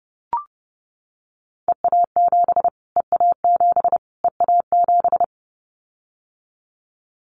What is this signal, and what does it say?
Morse code, heard as a keyed tone, about 30 wpm, 700 Hz. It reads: EA7 EA7 EA7
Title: Callsign Prefixes for Top 25 DXCC - Repeated 3x - 30wpm